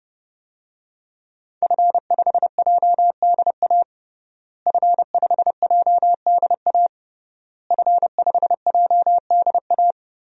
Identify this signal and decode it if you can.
Morse code, heard as a keyed tone, about 30 wpm, 700 Hz. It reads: F5JDA F5JDA F5JDA
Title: DXCC-100 Call Signs - 5 Characters - Repeated 3x - 30wpm